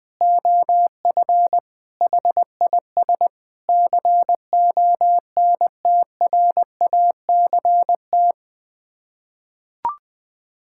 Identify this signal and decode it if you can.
Morse code, heard as a keyed tone, about 20 wpm, 700 Hz. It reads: OF HIS CONTRACT
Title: Sets of 8 Words - 20wpm